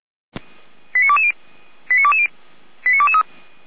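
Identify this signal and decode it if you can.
Five-tone selective call (ZVEI), heard as quick runs of five tones, tones of about 70 ms. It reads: 89220 89220 89223
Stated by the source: ZVEI five-tone call received on SDR.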